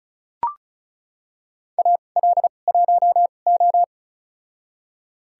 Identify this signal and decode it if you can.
Morse code, heard as a keyed tone, about 36 wpm, spaced as for 25 wpm, 700 Hz. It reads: AL1O
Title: USA Call Signs - 4 Characters - 2x Character Spacing - 35wpm